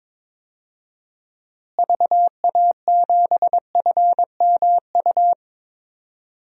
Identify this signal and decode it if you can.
Morse code, heard as a keyed tone, about 22 wpm, 700 Hz. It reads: VA7FMU
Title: DXCC-100 Call Signs - 6 Characters - Repeated 2x - 22wpm